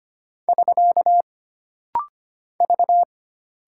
Morse code, heard as a keyed tone, about 25 wpm, 700 Hz. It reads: <SK> V
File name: Single Character (Letter:Number:Punctuation:Prosign) - Rapid-Fire - 25wpm